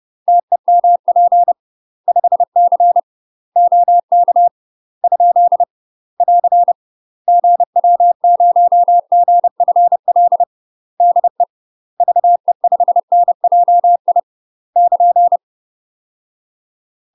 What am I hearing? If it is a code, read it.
Morse code, heard as a keyed tone, about 30 wpm, 700 Hz. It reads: TEMP 5C OK ? <AR> GW0GFL DE VE5NJI <KN>
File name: UK MASTER QSOs TRAINING / 30WPM / UK MASTER QSOs 0035